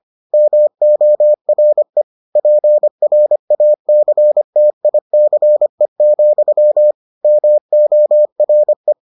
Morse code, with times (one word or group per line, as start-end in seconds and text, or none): 0.3-2.0 MORE
2.4-6.9 PRACTICE,
7.2-8.9 MORE